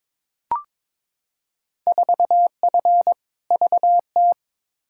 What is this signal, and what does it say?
Morse code, heard as a keyed tone, about 22 wpm, 700 Hz. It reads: 4F VT
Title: Field Day Exchanges Encore - 22wpm